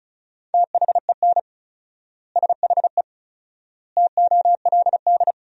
Morse code, heard as a keyed tone, about 36 wpm, 700 Hz. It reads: THEN SHE TOLD
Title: 4x Word-Spacing - Sentences from Top 400 Words - 35wpm